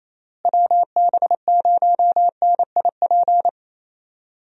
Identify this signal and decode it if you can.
Morse code, heard as a keyed tone, about 28 wpm, 700 Hz. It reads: WB0NIP
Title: USA Call Signs - 6 Characters - Repeated 2x - 28wpm